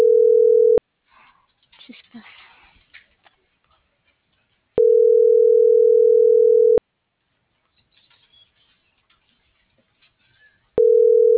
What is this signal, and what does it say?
Ringback tone (North America), 440 + 480 Hz.